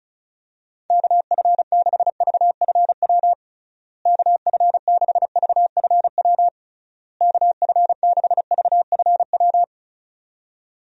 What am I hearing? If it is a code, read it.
Morse code, heard as a keyed tone, about 34 wpm, 700 Hz. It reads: KF6VFW KF6VFW KF6VFW
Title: USA Call Signs - 6 Characters - Repeated 3x - 35wpm